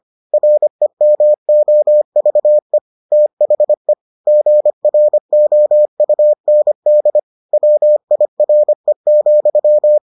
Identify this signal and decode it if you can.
Morse code, heard as a keyed tone, about 25 wpm, 600 Hz. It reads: REMOVE THE GROUND WIRE,